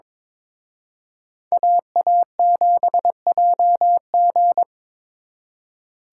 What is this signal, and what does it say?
Morse code, heard as a keyed tone, about 26 wpm, 700 Hz. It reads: AA7JG
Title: Speed Racing - USA 2x2 Call Signs 22wpm